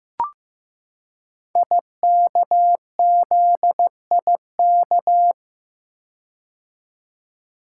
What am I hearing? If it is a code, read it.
Morse code, heard as a keyed tone, about 16 wpm, 700 Hz. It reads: IKZIK
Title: Code Groups - 5 Letters - 15wpm